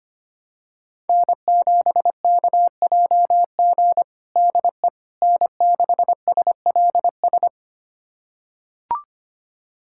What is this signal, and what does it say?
Morse code, heard as a keyed tone, about 25 wpm, 700 Hz. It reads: N7KJG DE N6SLS
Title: USA Callsign DE Callsign - 5 Characters - 25wpm